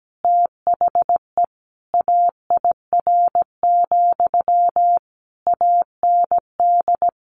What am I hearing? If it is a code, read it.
Morse code, heard as a keyed tone, about 17 wpm, 700 Hz. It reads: THE AIR, AND
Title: A Sound Of Thunder 17wpm